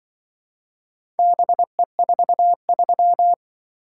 Morse code, heard as a keyed tone, about 24 wpm, 700 Hz. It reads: BE43
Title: Speed-Racing - Four Letter-Number Combinations - 20wpm